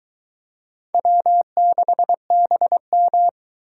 Morse code, heard as a keyed tone, about 23 wpm, 700 Hz. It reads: W6BM